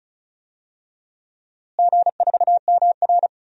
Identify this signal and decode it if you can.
Morse code, heard as a keyed tone, about 36 wpm, 700 Hz. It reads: G4MR
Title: DXCC-100 Call Signs - 4 Characters - Repeated 3x - 35wpm